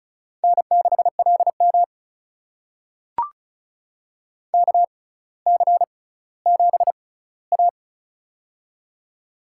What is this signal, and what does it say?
Morse code, heard as a keyed tone, about 36 wpm, spaced as for 18 wpm, 700 Hz. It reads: N6LM KC7A
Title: USA Call Signs - 4 Characters - 6x Character Spacing - 35wpm